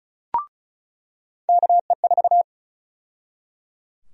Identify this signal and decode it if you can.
Morse code, heard as a keyed tone, about 34 wpm, 700 Hz. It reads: KE4
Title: USA Call Sign Prefixes - 35wpm